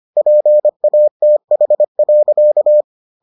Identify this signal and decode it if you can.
Morse code, heard as a keyed tone, about 25 wpm, 600 Hz. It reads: PATH.